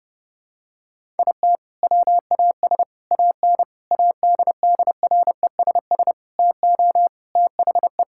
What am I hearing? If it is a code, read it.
Morse code, heard as a keyed tone, about 30 wpm, 700 Hz. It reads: IT WAS AN ADDRESS TO THE
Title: Sentences from Top 1000 Words - Review Part 6 - 30wpm